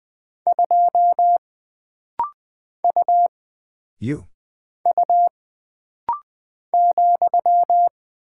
Morse code, heard as a keyed tone, about 20 wpm, 700 Hz. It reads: 2 U U ,
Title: Single Character (Letter:Number:Punctuation:Prosign) - Rapid-Fire - 20wpm